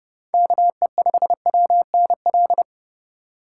Morse code, heard as a keyed tone, about 30 wpm, 700 Hz. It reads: KE5WNL